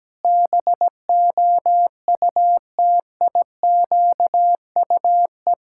Morse code, heard as a keyed tone, about 17 wpm, 700 Hz. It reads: BOUTIQUE
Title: Pangram Challenge v2 - 17wpm